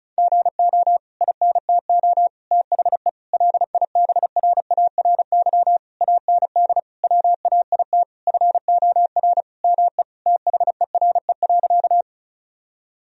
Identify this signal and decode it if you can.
Morse code, heard as a keyed tone, about 36 wpm, 700 Hz. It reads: GO INTO THE LIBRARY AND WAIT FOR ME THERE.
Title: Sentences from Top 1000 Words - Review Part 4 - 35wpm